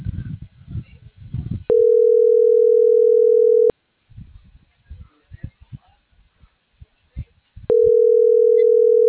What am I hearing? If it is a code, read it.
Ringback tone (North America), 440 + 480 Hz.